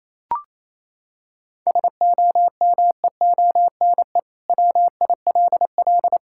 Morse code, heard as a keyed tone, about 28 wpm, 700 Hz. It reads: SOMEONE WILL
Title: Sentences from Top 1000 Words - Review Part 6 - 28wpm